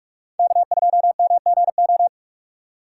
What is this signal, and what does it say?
Morse code, heard as a keyed tone, about 43 wpm, 700 Hz. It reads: K2MGO
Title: USA Call Signs - 5 Characters - 6x Character Spacing - 45wpm